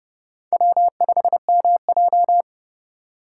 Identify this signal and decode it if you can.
Morse code, heard as a keyed tone, about 30 wpm, 700 Hz. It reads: W5MJ
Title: USA Call Signs - 4 Characters - 2x Character Spacing - 30wpm